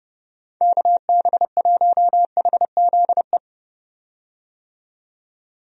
Morse code, heard as a keyed tone, about 30 wpm, 700 Hz. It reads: KB1HZE
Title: USA Call Signs - 6 Characters - 30wpm